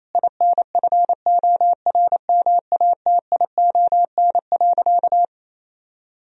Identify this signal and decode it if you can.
Morse code, heard as a keyed tone, about 28 wpm, 700 Hz. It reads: INFORMATION.
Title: Sentences from Top 1000 Words - Review Part 4 - 28wpm